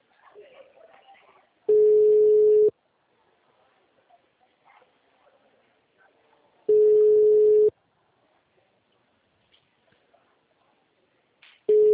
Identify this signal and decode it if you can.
Ringback tone (Europe, 425 Hz), with speech.